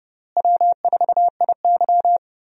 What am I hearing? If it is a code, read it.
Morse code, heard as a keyed tone, about 30 wpm, 700 Hz. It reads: W4IY
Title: USA Call Signs - 4 Characters - Repeated 3x - 30wpm